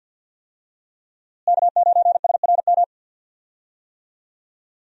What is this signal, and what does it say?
Morse code, heard as a keyed tone, about 52 wpm, 700 Hz. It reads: K9SRM
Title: USA Call Signs - 5 Characters - Repeated 2x - 50wpm